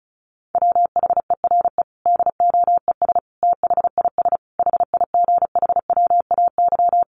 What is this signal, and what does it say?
Morse code, heard as a keyed tone, about 34 wpm, 700 Hz. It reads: WHERE DOES THIS HIGHWAY
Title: Sentences from Top 1400 Words 35wpm